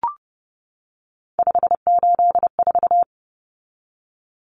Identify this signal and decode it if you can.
Morse code, heard as a keyed tone, about 30 wpm, 700 Hz. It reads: 584
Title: Numbers - 3-Digit 30wpm